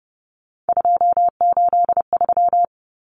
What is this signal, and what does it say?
Morse code, heard as a keyed tone, about 30 wpm, 700 Hz. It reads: 283
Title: Common Cut Numbers - 3-Digit 30wpm